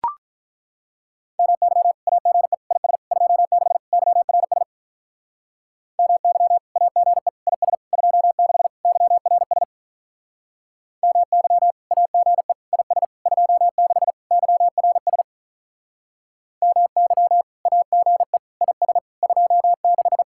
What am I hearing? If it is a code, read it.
Morse code, heard as a keyed tone, about 43 wpm, 700 Hz. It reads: MY AGE IS 26 YRS MY AGE IS 26 YRS MY AGE IS 26 YRS MY AGE IS 26
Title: Speed-Racing Common QSO Phrases - 35wpm